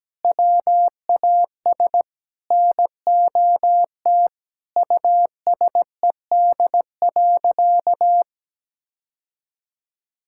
Morse code, heard as a keyed tone, about 17 wpm, 700 Hz. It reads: WAS NOT USED.
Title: Sentences from Top 1600 Words 17wpm